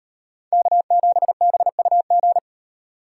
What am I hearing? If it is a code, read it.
Morse code, heard as a keyed tone, about 38 wpm, 700 Hz. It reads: K7BUG